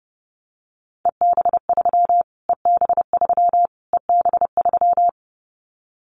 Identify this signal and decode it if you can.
Morse code, heard as a keyed tone, about 30 wpm, 700 Hz. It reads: EB3 EB3 EB3
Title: Callsign Prefixes for Top 25 DXCC - Repeated 3x - 30wpm